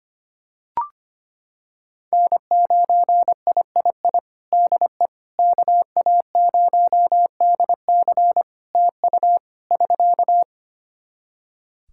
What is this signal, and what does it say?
Morse code, heard as a keyed tone, about 25 wpm, 700 Hz. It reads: N9III DE KA0DC TU <SK>
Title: Practice QSO - Exchange 4 - Ending - 25wpm